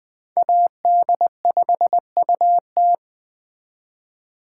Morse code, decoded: AD5UT

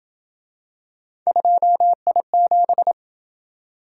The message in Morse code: 2I7

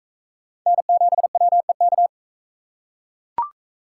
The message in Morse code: N7WEK

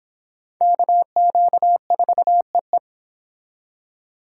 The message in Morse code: KQ4EE

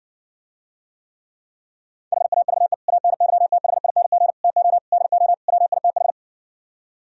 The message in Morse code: HAVE ANYTHING TO DO WITH

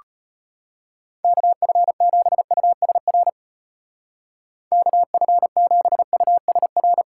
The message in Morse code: KF7USR KF7USR

